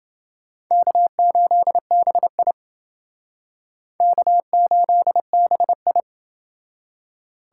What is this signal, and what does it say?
Morse code, heard as a keyed tone, about 28 wpm, 700 Hz. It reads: K8BI K8BI